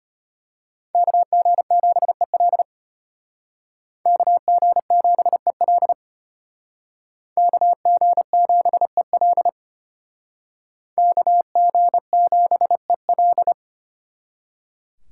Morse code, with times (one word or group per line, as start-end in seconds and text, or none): 0.9-2.6 KG7EL
4.1-5.9 KG7EL
7.4-9.5 KG7EL
11.0-13.5 KG7EL